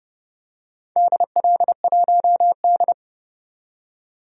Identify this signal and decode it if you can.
Morse code, heard as a keyed tone, about 29 wpm, 700 Hz. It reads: DL1D
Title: DXCC-100 Call Signs - 4 Characters - 30wpm